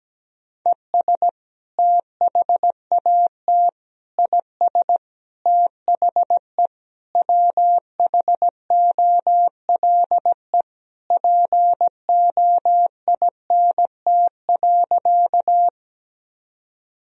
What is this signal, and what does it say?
Morse code, heard as a keyed tone, about 17 wpm, 700 Hz. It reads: ES THAT IS THE WHOLE POINT.
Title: Sentences from Top 400 Words and Abbreviations 17wpm